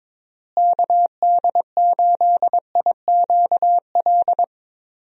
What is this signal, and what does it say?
Morse code, heard as a keyed tone, about 21 wpm, 700 Hz. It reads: KD8IQL
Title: USA Call Signs - 6 Characters - Repeated 3x - 22wpm